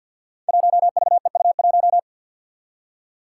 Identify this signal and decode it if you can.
Morse code, heard as a keyed tone, about 53 wpm, 700 Hz. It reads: 1VEU1
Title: Five Challenging Letter-Number Combinations 50wpm